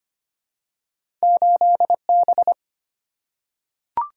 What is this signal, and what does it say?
Morse code, heard as a keyed tone, about 25 wpm, 700 Hz. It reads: 8 B
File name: Two Letter-Number Combinations - 25wpm